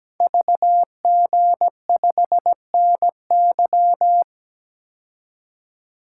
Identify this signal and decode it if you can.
Morse code, heard as a keyed tone, about 17 wpm, 700 Hz. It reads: VG5NY
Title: Code Groups - 5 Letter:Numbers - 17wpm